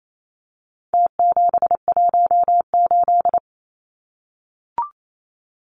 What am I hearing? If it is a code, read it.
Morse code, heard as a keyed tone, about 28 wpm, 700 Hz. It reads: T718
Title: Numbers - Contest Style - 4-Digit 28wpm